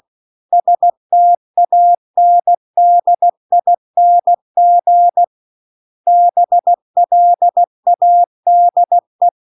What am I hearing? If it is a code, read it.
Morse code, heard as a keyed tone, about 16 wpm, 700 Hz. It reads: STANDING BLADE
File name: CW-The-Big-Twoheaded-River-2-16WPM